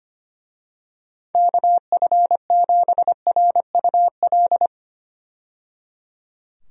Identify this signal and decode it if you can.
Morse code, heard as a keyed tone, about 25 wpm, 700 Hz. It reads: KF7RUL
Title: USA Call Signs - 6 Characters - 25wpm